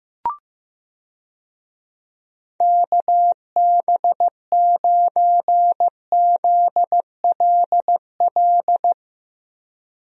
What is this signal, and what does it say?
Morse code, heard as a keyed tone, about 16 wpm, 700 Hz. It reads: KB9ZLL
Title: USA Call Signs - 6 Characters - Repeated 3x - 15wpm